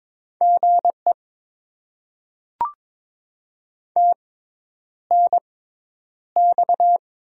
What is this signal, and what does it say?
Morse code, heard as a keyed tone, about 22 wpm, spaced as for 10 wpm, 700 Hz. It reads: GE TNX
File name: Top 20 QSO Elements - 6x Character Spacing - 22wpm